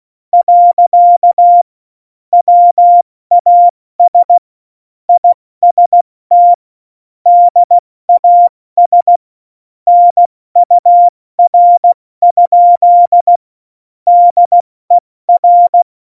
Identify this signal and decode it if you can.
Morse code, heard as a keyed tone, about 16 wpm, 700 Hz. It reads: . WAS IST DAS NUR? DER
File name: Morsekurs / Witze / Witz 09. Januar 2018, 70 F80